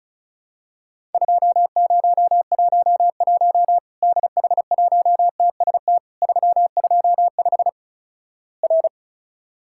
Morse code, heard as a keyed tone, about 36 wpm, 700 Hz. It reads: 2011 DH1TST 325 R
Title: Worked All Europe DX Contest - QTC Exchange - 35wpm